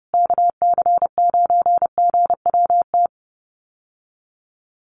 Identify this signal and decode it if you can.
Morse code, heard as a keyed tone, about 30 wpm, 700 Hz. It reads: KC9GWT
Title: USA Call Signs Repeated 3x 30wpm